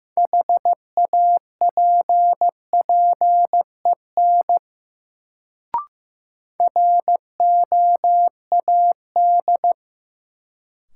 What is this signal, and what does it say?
Morse code, heard as a keyed tone, about 15 wpm, 700 Hz. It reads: HAPPEN ROAD
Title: Sentences from Top 400 Words - New Word Drill - 15wpm